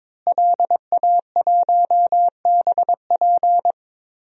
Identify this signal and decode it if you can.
Morse code, heard as a keyed tone, about 21 wpm, 700 Hz. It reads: LA1BP
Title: DXCC-100 Call Signs - 5 Characters - Repeated 3x - 22wpm